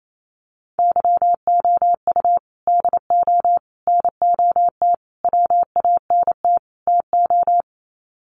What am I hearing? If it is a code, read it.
Morse code, heard as a keyed tone, about 28 wpm, 700 Hz. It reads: YOU DO NOT WANT TO
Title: Sets of 5 Words 28wpm